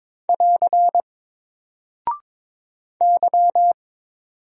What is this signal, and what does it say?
Morse code, heard as a keyed tone, about 22 wpm, 700 Hz. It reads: <AR> Y
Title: Single Character (Letter:Number:Punctuation:Prosign) 22wpm